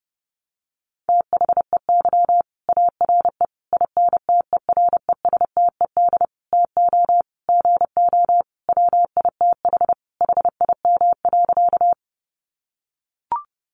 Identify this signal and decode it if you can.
Morse code, heard as a keyed tone, about 30 wpm, 700 Hz. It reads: THEY ARE INTERESTED TO GO WITH HIM.
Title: Sentences from Top 900 Words 30wpm